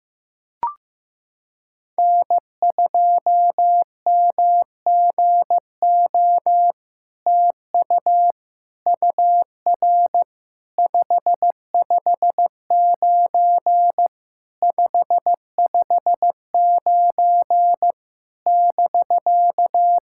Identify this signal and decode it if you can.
Morse code, heard as a keyed tone, about 15 wpm, 700 Hz. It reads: N2MGO TU UR 559 559 <BK>
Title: SOTA QSO Practice 15wpm